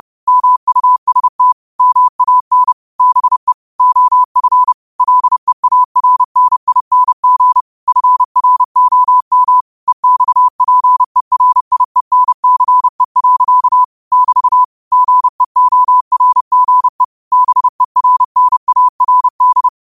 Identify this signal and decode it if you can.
Morse code, heard as a keyed tone, about 30 wpm, 1000 Hz. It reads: MUST MAN BE OF LEARNING FROM EXPERIENCE. <BT> GEORGE BERNARD
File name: qotdfiles / 30wpm